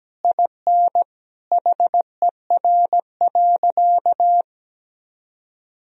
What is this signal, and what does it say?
Morse code, heard as a keyed tone, about 17 wpm, 700 Hz. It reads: IN HER.
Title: Sentences from Top 1000 Words - Review Part 4 - 17wpm